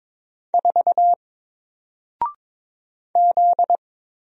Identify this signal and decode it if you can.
Morse code, heard as a keyed tone, about 22 wpm, 700 Hz. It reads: 4 Z
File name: Single-Letter-Number-22wpm